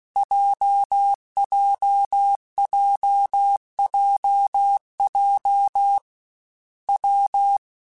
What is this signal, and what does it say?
Morse code, heard as a keyed tone, about 16 wpm, 800 Hz. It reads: JJJJJ W
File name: Difficult Combos / Difficult Combos-mp3s / WPJ WPJ1 @15